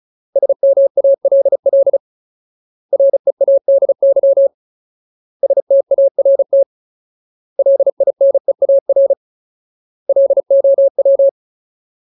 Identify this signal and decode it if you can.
Morse code, heard as a keyed tone, about 34 wpm, 550 Hz. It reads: SMALL READY START LINEAR LOW
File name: Ultimate Word List / 35WPM Ultimate